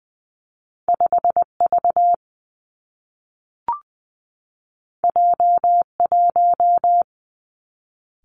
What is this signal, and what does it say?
Morse code, heard as a keyed tone, about 20 wpm, 700 Hz. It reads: 5 V J 1
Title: Two Challenging Letter-Number Combinations 20wpm